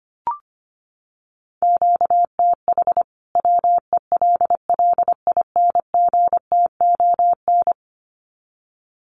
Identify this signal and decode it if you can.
Morse code, heard as a keyed tone, about 25 wpm, 700 Hz. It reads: QTH WELLINGTON
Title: Common QSO Phrases 25wpm